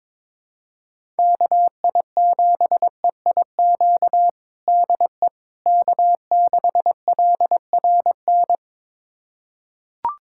Morse code, heard as a keyed tone, about 21 wpm, 700 Hz. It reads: KI7EIQ DE K6LRN
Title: USA Callsign DE Callsign 22wpm